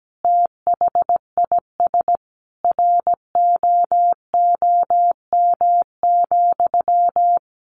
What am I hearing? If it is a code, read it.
Morse code, heard as a keyed tone, about 17 wpm, 700 Hz. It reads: THIS ROOM,
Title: Sentences from Top 600 Words 17wpm